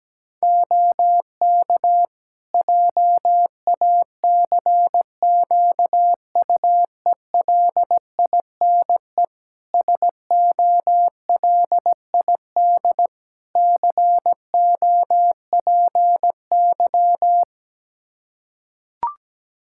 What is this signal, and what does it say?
Morse code, heard as a keyed tone, about 17 wpm, 700 Hz. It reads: OK JACQUELINE SOLID COPY
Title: K1USN SST QSO Practice 17wpm